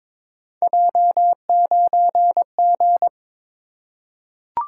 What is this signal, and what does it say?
Morse code, heard as a keyed tone, about 21 wpm, 700 Hz. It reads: J9G
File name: practice / 22wpm / Three Letter-Number Combinations - 22wpm